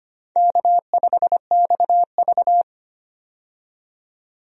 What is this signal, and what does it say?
Morse code, heard as a keyed tone, about 25 wpm, 700 Hz. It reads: K5XV
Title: USA Call Signs - 4 Characters - 25wpm